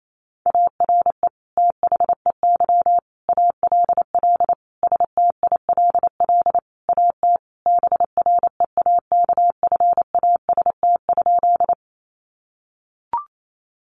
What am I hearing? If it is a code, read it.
Morse code, heard as a keyed tone, about 28 wpm, 700 Hz. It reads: ARE THEY ALL STILL AT BREAKFAST?
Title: Sentences from Top 1500 Words 28wpm